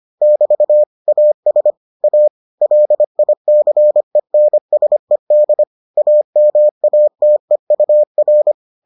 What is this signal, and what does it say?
Morse code, heard as a keyed tone, about 25 wpm, 600 Hz. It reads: <BT> AS A LICENSED AMATEUR